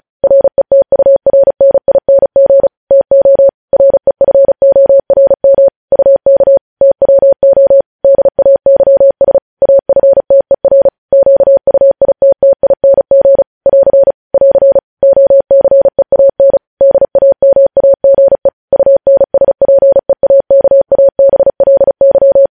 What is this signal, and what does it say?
Morse code, heard as a keyed tone, about 36 wpm, 550 Hz. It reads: RETURNING TO REFORM UK TWO DAYS AFTER QUITTING <AR> <AR> OCEAN DAMAGE UNSPEAKABLY
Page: news headlines in morse code at 35 wpm